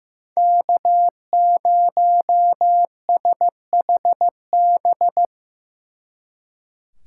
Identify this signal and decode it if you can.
Morse code, heard as a keyed tone, about 17 wpm, 700 Hz. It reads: K0SHB
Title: Speed Racing - USA 1x3 Call Signs 15wpm